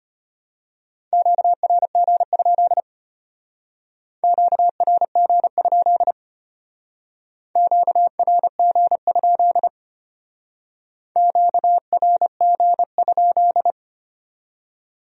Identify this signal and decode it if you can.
Morse code, heard as a keyed tone, about 32 wpm, 700 Hz. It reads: QRG? QRG? QRG? QRG?